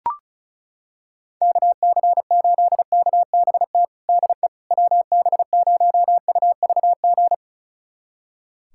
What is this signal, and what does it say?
Morse code, heard as a keyed tone, about 36 wpm, 700 Hz. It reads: KC8KBT DE WB0UVG
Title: USA Callsign DE Callsign - 6 Characters - 35wpm